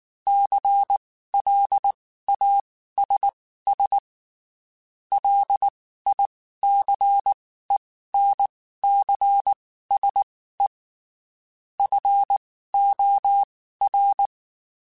Morse code, heard as a keyed tone, about 19 wpm, spaced as for 14 wpm, 800 Hz. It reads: CLASS LICENCSE FOR
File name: (spaced) / QSOs Spaced-mp3s / kc7sac @18 sp